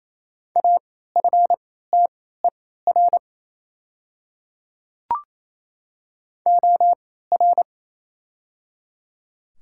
Morse code, heard as a keyed tone, about 28 wpm, spaced as for 16 wpm, 700 Hz. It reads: AFTER OR
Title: 3x Character Spacing - Top 200 Words - 28wpm